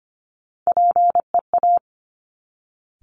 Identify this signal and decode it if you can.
Morse code, heard as a keyed tone, about 25 wpm, 700 Hz. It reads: PEA
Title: Top Three Letter Combinations in Words 25wpm